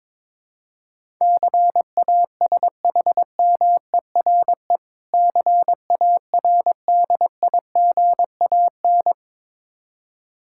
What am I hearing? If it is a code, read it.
Morse code, heard as a keyed tone, about 22 wpm, 700 Hz. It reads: CASHMERE CARDIGAN
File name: Difficult Words with Hints - 22wpm